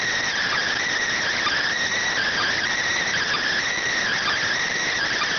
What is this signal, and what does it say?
Emergency Alert System (SAME) tones, heard as harsh buzzing bursts. It is heard from 0.4 to 4.4 s.